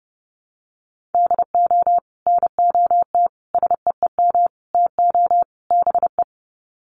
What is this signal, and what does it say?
Morse code, heard as a keyed tone, about 30 wpm, 700 Hz. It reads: DO NOT SEEM TO BE
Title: Sets of 5 Words Encore2 30wpm